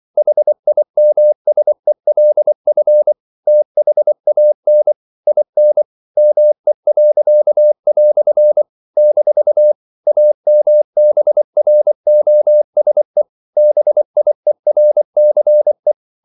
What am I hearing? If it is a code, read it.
Morse code, heard as a keyed tone, about 24 wpm, 600 Hz. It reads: HIMSELF THAN IN ME." - AMBROSE BIERCE
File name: CW / Practice / EnglishQuotations0224